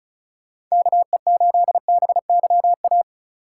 Morse code, heard as a keyed tone, about 36 wpm, 700 Hz. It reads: KE8BYA
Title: USA Call Signs - 6 Characters - 8x Character Spacing - 35wpm